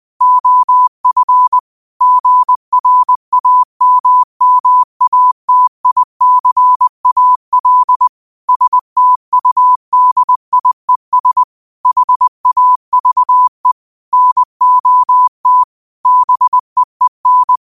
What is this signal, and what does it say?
Morse code, heard as a keyed tone, about 20 wpm, 1000 Hz. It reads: OF GRAMMATICAL STUDIES HAVE NOT BEEN
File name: qotdfiles / 20wpm